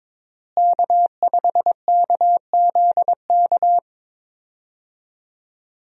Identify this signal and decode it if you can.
Morse code, heard as a keyed tone, about 23 wpm, 700 Hz. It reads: K5KZK
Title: USA Call Signs - 5 Characters - Repeated 3x - 22wpm